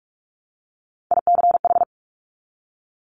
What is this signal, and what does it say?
Morse code, heard as a keyed tone, about 46 wpm, 700 Hz. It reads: ICH
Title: Top Three Letter Combinations in Words 45wpm